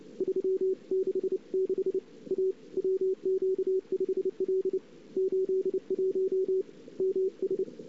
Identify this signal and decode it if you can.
Morse code, heard as a keyed tone, about 30 wpm, 410 Hz. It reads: 366 A WQ5L 81 MS
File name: CW / 2008-ss-cw-k5tr-op-n5rz -06:00